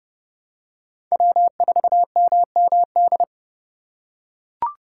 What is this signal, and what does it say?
Morse code, heard as a keyed tone, about 30 wpm, 700 Hz. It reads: W4MMD
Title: K1USN SST QSO Practice - 30wpm